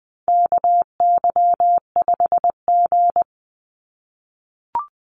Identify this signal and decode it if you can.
Morse code, heard as a keyed tone, about 19 wpm, 700 Hz. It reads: KY5G
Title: USA Call Signs 20wpm